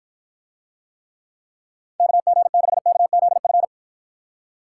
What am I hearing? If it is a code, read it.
Morse code, heard as a keyed tone, about 53 wpm, 700 Hz. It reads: KG6KZF